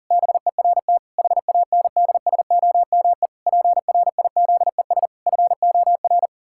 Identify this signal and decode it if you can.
Morse code, heard as a keyed tone, about 37 wpm, 700 Hz. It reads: BERT HANDSOME PRIZES FOR
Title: Pangram Challenge v2 - 40wpm